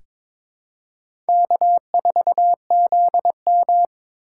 Morse code, heard as a keyed tone, about 23 wpm, 700 Hz. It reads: K4ZM
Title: USA Call Signs - 4 Characters - 3x Character Spacing - 22wpm